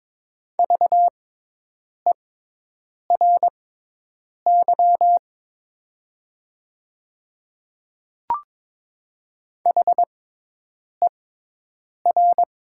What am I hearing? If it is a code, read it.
Morse code, heard as a keyed tone, about 22 wpm, spaced as for 8 wpm, 690 Hz. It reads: VERY HER
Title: 6x Character Spacing - Top 200 Words - 22wpm